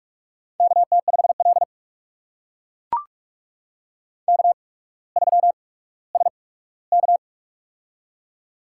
Morse code, heard as a keyed tone, about 44 wpm, spaced as for 19 wpm, 700 Hz. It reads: KT5L K3SK
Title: USA Call Signs - 4 Characters - 8x Character Spacing - 45wpm